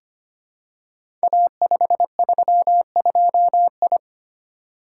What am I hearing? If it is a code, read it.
Morse code, heard as a keyed tone, about 25 wpm, 700 Hz. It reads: A532I
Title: Code Groups - 5 Letter:Numbers - 25wpm